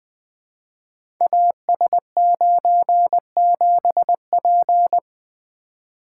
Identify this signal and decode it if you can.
Morse code, heard as a keyed tone, about 20 wpm, 700 Hz. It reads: AS97P
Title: Code Groups - 5 Letter:Numbers - 20wpm